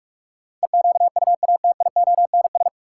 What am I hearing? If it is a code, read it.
Morse code, heard as a keyed tone, about 47 wpm, 700 Hz. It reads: EQUATIONS
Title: Sets of 5 Words - New Word Drill - 45wpm